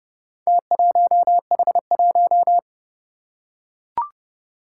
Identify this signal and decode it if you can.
Morse code, heard as a keyed tone, about 31 wpm, 700 Hz. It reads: T1H1
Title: Four Challenging Letter-Number Combinations 30wpm